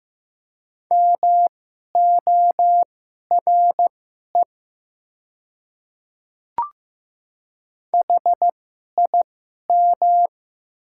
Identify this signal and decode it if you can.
Morse code, heard as a keyed tone, about 15 wpm, spaced as for 11 wpm, 690 Hz. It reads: MORE HIM
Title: 2x Character Spacing - Top 100 Words - 15wpm